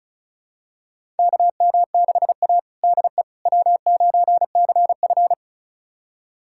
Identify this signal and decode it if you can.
Morse code, heard as a keyed tone, about 36 wpm, 700 Hz. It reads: KM6A DE W9CF
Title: USA Callsign DE Callsign - 4 Characters - 35wpm